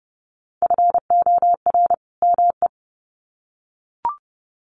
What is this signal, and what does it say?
Morse code, heard as a keyed tone, about 30 wpm, 700 Hz. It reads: FOR ME
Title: Sets of 2 Words 30wpm